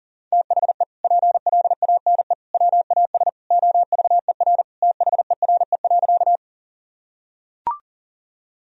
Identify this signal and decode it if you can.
Morse code, heard as a keyed tone, about 40 wpm, 700 Hz. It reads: THE PLANE WAS OVER THERE.
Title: Sentences from Top 1000 Words - Review Part 3 - 40wpm